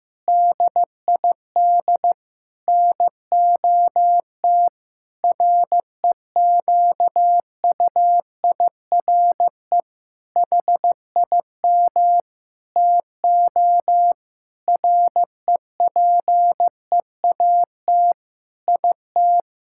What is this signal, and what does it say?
Morse code, heard as a keyed tone, about 15 wpm, 700 Hz. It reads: DID NOT REQUIRE HIM TO REPEAT IT
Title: Sentences from Top 1600 Words 15wpm